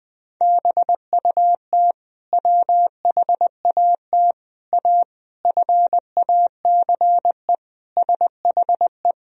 Morse code, heard as a keyed tone, about 21 wpm, 700 Hz. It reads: BUT WHAT A FACE SHE
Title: Sentences from Top 1000 Words - Review Part 2 - 20wpm